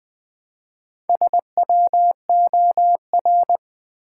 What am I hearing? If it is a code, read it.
Morse code, heard as a keyed tone, about 20 wpm, 700 Hz. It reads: SWOR
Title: Build A Word - 1 Letter At A Time - 5-Letter Words - 20wpm